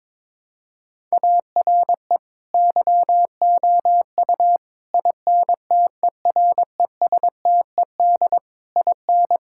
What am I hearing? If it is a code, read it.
Morse code, heard as a keyed tone, about 22 wpm, 700 Hz. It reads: ARE YOU INTERESTED IN
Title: Sentences from Top 1000 Words - Review Part 6 - 22wpm